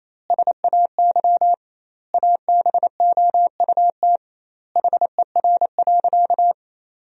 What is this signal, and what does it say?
Morse code, heard as a keyed tone, about 28 wpm, 700 Hz. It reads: SAY ABOUT HER.
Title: 2x Word-Spacing - Sentences from Top 200 Words - 28wpm